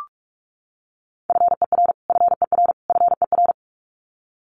Morse code, heard as a keyed tone, about 44 wpm, 700 Hz. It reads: FER FER FER
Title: Abbreviations Repeated 3x - 45wpm